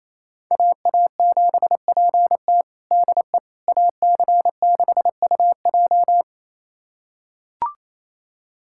Morse code, heard as a keyed tone, about 27 wpm, 700 Hz. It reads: AA7PT DE AC6UJ
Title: USA Callsign DE Callsign - 5 Characters - 28wpm